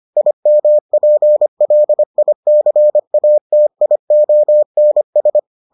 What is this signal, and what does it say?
Morse code, heard as a keyed tone, about 25 wpm, 600 Hz. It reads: IMPLICATIONS